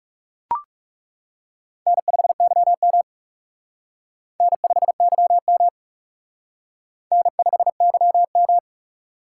Morse code, heard as a keyed tone, about 36 wpm, 700 Hz. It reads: N5YM N5YM N5YM
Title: Speed Racing - USA 1x2 Call Signs 30wpm